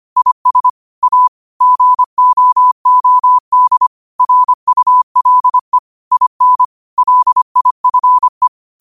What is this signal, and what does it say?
Morse code, heard as a keyed tone, about 25 wpm, 1000 Hz. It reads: IS A GOOD RULE IN LIFE